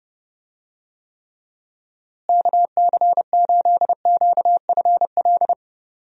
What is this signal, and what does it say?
Morse code, heard as a keyed tone, about 30 wpm, 700 Hz. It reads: KC8QFL
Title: USA Call Signs - 6 Characters - Repeated 3x - 30wpm